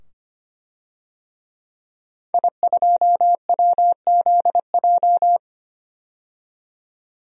Morse code, heard as a keyed tone, about 25 wpm, 700 Hz. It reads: I2WZJ
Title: DXCC-100 Call Signs - 5 Characters - 25wpm